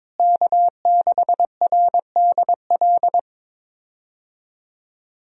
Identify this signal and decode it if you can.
Morse code, heard as a keyed tone, about 21 wpm, 700 Hz. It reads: K6RDL
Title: USA Call Signs - 5 Characters - Repeated 3x - 22wpm